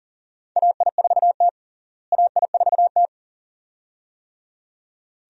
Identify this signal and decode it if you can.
Morse code, heard as a keyed tone, about 40 wpm, 700 Hz. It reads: AI4T AI4T
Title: USA Call Signs - 4 Characters - Repeated 2x - 40wpm